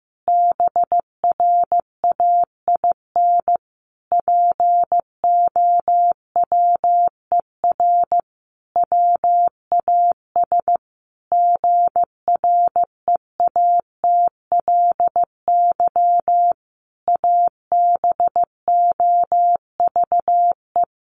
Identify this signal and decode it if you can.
Morse code, heard as a keyed tone, about 15 wpm, 700 Hz. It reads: BRAIN POWER WAS GREATLY ABOVE
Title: Sentences from Top 1400 Words 15wpm